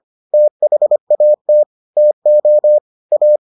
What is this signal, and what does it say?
Morse code, heard as a keyed tone, about 25 wpm, 600 Hz. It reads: THAT TO A